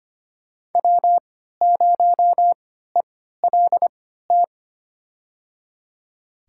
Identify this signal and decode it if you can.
Morse code, heard as a keyed tone, about 25 wpm, spaced as for 14 wpm, 700 Hz. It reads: W0ELT
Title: USA Call Signs - 3x Character Spacing - 25wpm